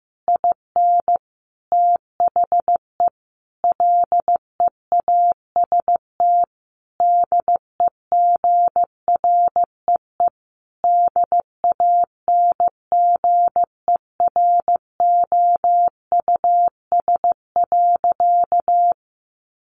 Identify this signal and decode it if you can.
Morse code, heard as a keyed tone, about 15 wpm, 700 Hz. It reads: IN THE LEAST DEGREE DANGEROUS.
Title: Sentences from Top 1500 Words 15wpm